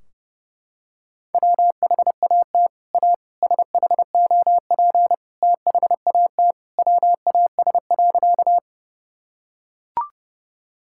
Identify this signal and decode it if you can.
Morse code, heard as a keyed tone, about 30 wpm, 700 Hz. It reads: WHAT A SHOP THAT WAS.
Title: Sentences from Top 1600 Words 30wpm